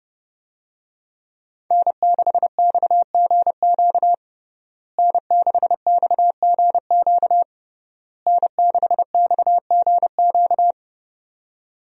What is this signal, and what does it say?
Morse code, heard as a keyed tone, about 32 wpm, 700 Hz. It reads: N6XGQ N6XGQ N6XGQ
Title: USA Call Signs - 5 Characters - Repeated 3x - 30wpm